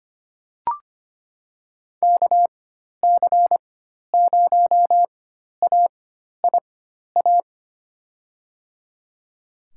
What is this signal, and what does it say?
Morse code, heard as a keyed tone, about 25 wpm, spaced as for 12 wpm, 700 Hz. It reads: KC0AIA
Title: USA Call Signs - 6 Characters - 4x Character Spacing - 25wpm